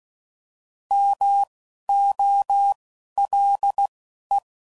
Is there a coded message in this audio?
Yes: Morse code from 0.9 to 4.4 s, heard as a keyed tone, about 16 wpm, spaced as for 11 wpm, 800 Hz. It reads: MOLE